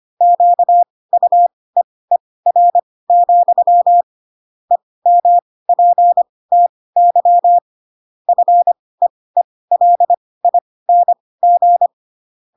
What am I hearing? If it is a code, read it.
Morse code, heard as a keyed tone, about 25 wpm, spaced as for 18 wpm, 700 Hz. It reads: QUEER, EMPTY FEELING